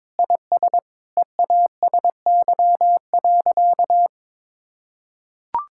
Morse code, heard as a keyed tone, about 22 wpm, 700 Hz. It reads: IS EASY.
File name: Sentences with 7-Word Phrases - 22wpm